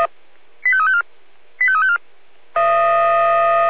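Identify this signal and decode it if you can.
Five-tone selective call (ZVEI), heard as quick runs of five tones, tones of about 80 ms. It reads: 85354 85354
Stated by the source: ZVEI five-tone selective call 85882, received on SDR.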